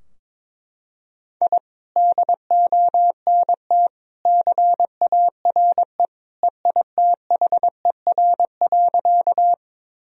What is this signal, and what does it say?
Morse code, heard as a keyed tone, about 22 wpm, 700 Hz. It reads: I DONT CARE EITHER.
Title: Sentences from Top 1000 Words - Review Part 4 - 22wpm